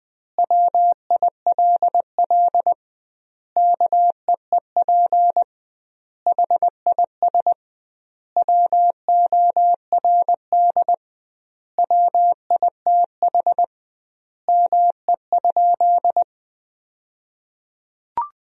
Morse code, heard as a keyed tone, about 20 wpm, 700 Hz. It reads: WILL KEEP HIS WORD WITH ME?
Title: 2x Word-Spacing - Sentences from Top 300 Words - 20wpm